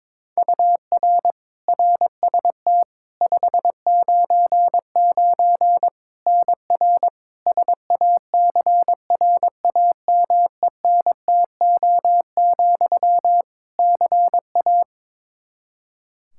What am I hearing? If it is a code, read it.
Morse code, heard as a keyed tone, about 22 wpm, 700 Hz. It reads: UR RST 599 NR SACRAMENTO, CA
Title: Practice QSO - Exchange 1 - Beginning - 22wpm